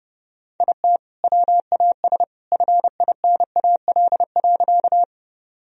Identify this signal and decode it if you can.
Morse code, heard as a keyed tone, about 32 wpm, 700 Hz. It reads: IT WAS FINAL.
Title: Sentences from Top 1000 Words - Review Part 2 - 30wpm